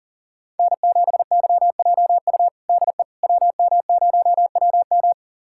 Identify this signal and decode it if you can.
Morse code, heard as a keyed tone, about 40 wpm, 700 Hz. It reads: N7YJU DE WM0WM
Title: USA Callsign DE Callsign - 5 Characters - 40wpm